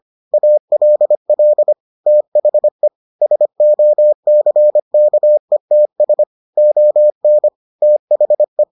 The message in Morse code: ALL THE SOCKETS ON THE

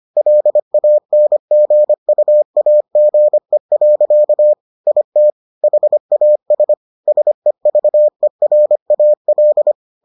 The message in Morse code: LANGUAGE. IT HAS SEVERAL

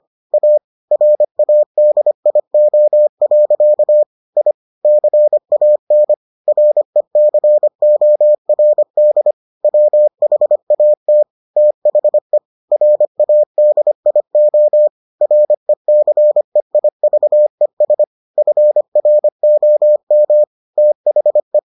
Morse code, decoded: A RADIO. I CAN RECORD WHAT THE RADIO RECEIVES FROM THE

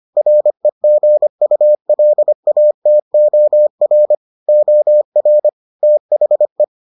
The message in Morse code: REGULATOR OR THE

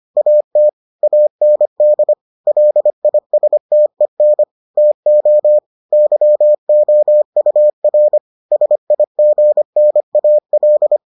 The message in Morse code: AT AND LISTEN TO YOUR SIGNAL

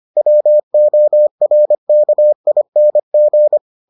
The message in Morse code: WORKING

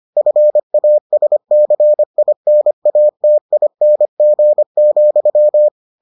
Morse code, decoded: FASCINATING,

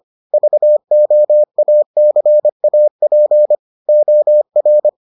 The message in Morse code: VOACAP OR

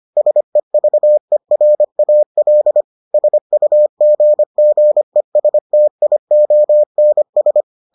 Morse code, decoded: SEVERAL SUGGESTIONS

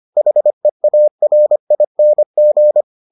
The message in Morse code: HEARING

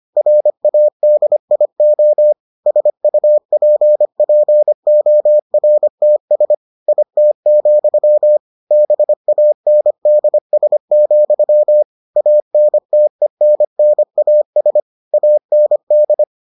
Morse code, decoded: RADIO SUPPORTS IT, BANDS, ANTENNAS AND

